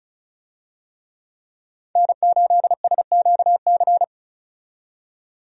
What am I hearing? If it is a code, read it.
Morse code, heard as a keyed tone, about 36 wpm, 700 Hz. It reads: N8SQC
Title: USA Call Signs - 5 Characters - 35wpm